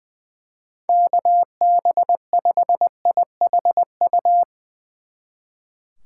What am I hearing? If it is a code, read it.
Morse code, heard as a keyed tone, about 20 wpm, 700 Hz. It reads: KB5IHU
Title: Speed Racing - USA Call Signs 20wpm